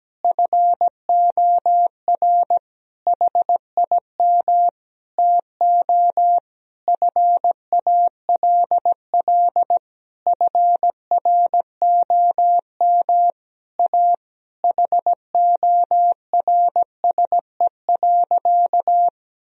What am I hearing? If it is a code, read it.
Morse code, heard as a keyed tone, about 17 wpm, 700 Hz. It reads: FOR HIM TO FALL FROM A HORSE.
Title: Sentences from Top 1000 Words - Review Part 2 - 17wpm